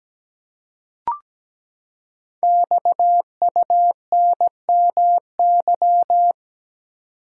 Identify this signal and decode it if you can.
Morse code, heard as a keyed tone, about 17 wpm, 700 Hz. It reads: XUNMY
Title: Code Groups - 5 Letters - 17wpm